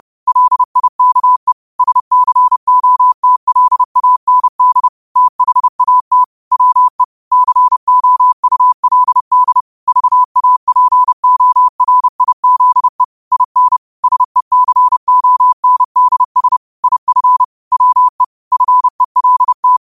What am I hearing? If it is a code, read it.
Morse code, heard as a keyed tone, about 30 wpm, 1000 Hz. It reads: LIKE SCOTLAND THAT WE COULD VAPORIZE IN SECONDS IF WE FELT